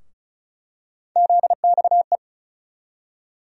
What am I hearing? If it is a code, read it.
Morse code, heard as a keyed tone, about 34 wpm, 700 Hz. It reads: ZXE
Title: USA Call Sign Postfixes - 3 Letters 35wpm